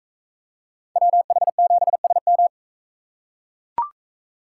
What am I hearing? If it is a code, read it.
Morse code, heard as a keyed tone, about 42 wpm, 700 Hz. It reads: WH7SM